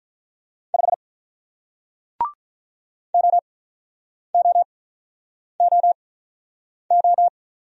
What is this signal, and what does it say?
Morse code, heard as a keyed tone, about 46 wpm, spaced as for 9 wpm, 700 Hz. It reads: 5 OOOO